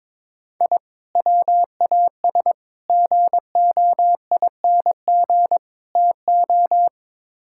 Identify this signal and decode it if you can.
Morse code, heard as a keyed tone, about 22 wpm, 700 Hz. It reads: I WAS GOING TO
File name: practice / 22wpm / Build A Phrase - 5-Word Phrases - 22wpm